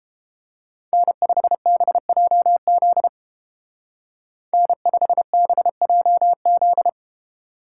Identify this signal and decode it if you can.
Morse code, heard as a keyed tone, about 31 wpm, 700 Hz. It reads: N5BJZ N5BJZ